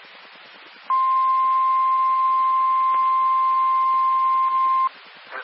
Emergency Alert System (SAME) tones (the 1050 Hz attention tone). It is heard from 0.8 to 4.9 s.
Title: Nassau and Suffolk County Amateur Repeaters